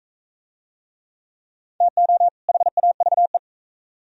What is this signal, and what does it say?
Morse code, heard as a keyed tone, about 40 wpm, 700 Hz. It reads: TO HAVE